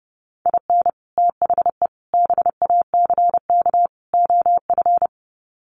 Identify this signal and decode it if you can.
Morse code, heard as a keyed tone, about 30 wpm, 700 Hz. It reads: IN THE BACK OF
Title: Sets of 4 Words 30wpm